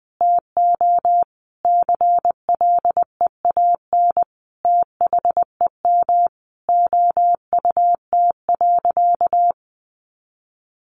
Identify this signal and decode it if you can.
Morse code, heard as a keyed tone, about 20 wpm, 700 Hz. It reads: TO CLEAN THEM OUT.
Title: Sentences from Top 1500 Words 20wpm